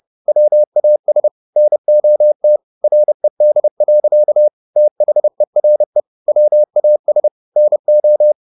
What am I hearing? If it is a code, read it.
Morse code, heard as a keyed tone, about 30 wpm, 600 Hz. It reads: WAS NOT RED. THERE WAS NO